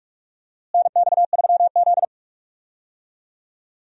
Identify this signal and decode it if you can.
Morse code, heard as a keyed tone, about 46 wpm, 700 Hz. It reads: NX3Z
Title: USA Call Signs - 4 Characters - Repeated 3x - 45wpm